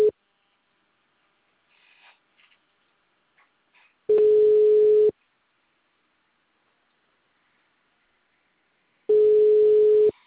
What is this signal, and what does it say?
Ringback tone (Europe, 425 Hz).
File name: out- -2012- - - .74478